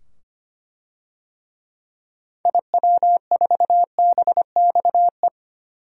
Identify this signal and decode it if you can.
Morse code, heard as a keyed tone, about 25 wpm, 700 Hz. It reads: IW4BXE
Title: DXCC-100 Call Signs - 6 Characters - 25wpm